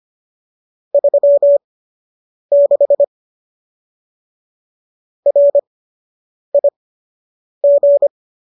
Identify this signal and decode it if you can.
Morse code, heard as a keyed tone, about 25 wpm, spaced as for 8 wpm, 570 Hz. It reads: 36 RIG
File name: qso006 08